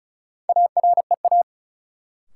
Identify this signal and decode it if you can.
Morse code, heard as a keyed tone, about 36 wpm, 700 Hz. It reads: AREA
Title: Sentences from Top 1000 Words - Review Part 2 - New Word Drill - 35wpm